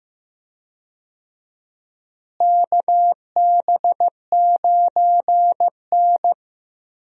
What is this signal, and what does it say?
Morse code, heard as a keyed tone, about 15 wpm, 700 Hz. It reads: KB9N
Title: USA Call Signs - 4 Characters - Repeated 2x - 15wpm